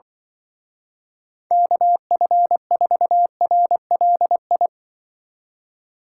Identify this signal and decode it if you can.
Morse code, heard as a keyed tone, about 24 wpm, 700 Hz. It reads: KF4RLI
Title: Speed Racing - USA Call Signs 20wpm